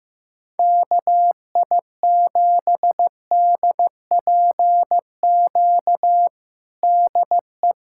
Morse code, heard as a keyed tone, about 15 wpm, 700 Hz. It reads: KI7DPQ DE
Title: USA Callsign DE Callsign - 6 Characters - 15wpm